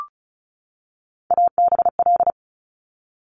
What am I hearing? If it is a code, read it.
Morse code, heard as a keyed tone, about 36 wpm, 700 Hz. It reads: ABL
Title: Top Three Letter Combinations in Words 35wpm